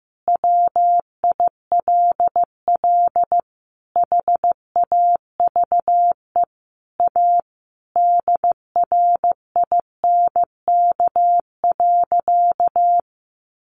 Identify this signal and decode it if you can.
Morse code, heard as a keyed tone, about 15 wpm, 700 Hz. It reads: WILL HAVE A DRINK.
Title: Sentences from Top 1500 Words 15wpm